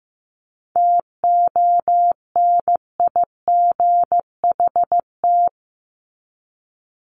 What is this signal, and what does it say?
Morse code, heard as a keyed tone, about 15 wpm, 700 Hz. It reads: TONIGHT
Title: Top 501-750 Words 15wpm